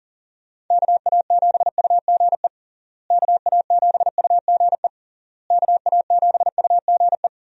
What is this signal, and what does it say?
Morse code, heard as a keyed tone, about 37 wpm, 700 Hz. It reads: KA7UGE KA7UGE KA7UGE
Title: USA Call Signs - 6 Characters - Repeated 3x - 40wpm